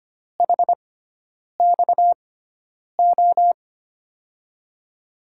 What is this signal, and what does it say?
Morse code, heard as a keyed tone, about 25 wpm, 700 Hz. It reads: H X O
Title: USA Call Signs - 6x Character Spacing - 25wpm